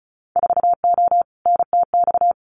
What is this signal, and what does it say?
Morse code, heard as a keyed tone, about 34 wpm, 700 Hz. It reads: 4O NTX
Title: Winter Field Day Exchanges 35wpm